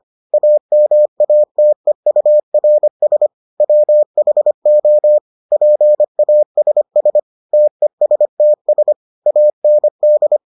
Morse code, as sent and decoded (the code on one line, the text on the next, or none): .- -- .- - . ..- .-. ... / .-- .... --- / .--. .- ... ... / - . ... - ... / .- -. -..
AMATEURS WHO PASS TESTS AND